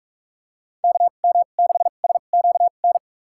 Morse code, heard as a keyed tone, about 46 wpm, spaced as for 32 wpm, 700 Hz. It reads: KM6SQN
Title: USA Call Signs - 6 Characters - 2x Character Spacing - 45wpm